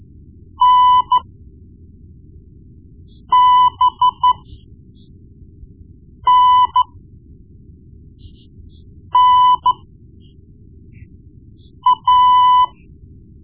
Morse code, heard as a keyed tone, about 12 wpm, spaced as for 4 wpm, 1000 Hz. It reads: NBNNA
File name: Morse Lessons / Morse Lessons / Tape-1B